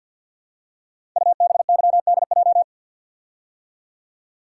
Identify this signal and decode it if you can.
Morse code, heard as a keyed tone, about 50 wpm, 700 Hz. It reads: UBYDJ
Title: Code Groups - 5 Letters - 50wpm